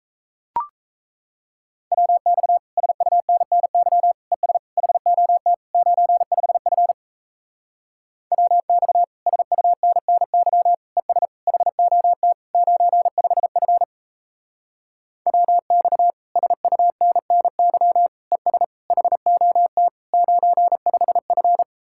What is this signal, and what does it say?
Morse code, heard as a keyed tone, about 37 wpm, 700 Hz. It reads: WX SUNNY ES HOT 95F WX SUNNY ES HOT 95F WX SUNNY ES HOT 95F